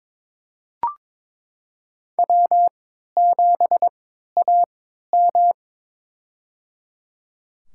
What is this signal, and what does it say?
Morse code, heard as a keyed tone, about 22 wpm, spaced as for 12 wpm, 700 Hz. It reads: W7AM
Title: USA Call Signs - 4 Characters - 3x Character Spacing - 22wpm